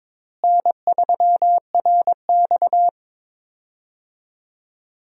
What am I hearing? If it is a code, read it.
Morse code, heard as a keyed tone, about 22 wpm, 700 Hz. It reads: N3RX
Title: USA Call Signs - 4 Characters - Repeated 3x - 22wpm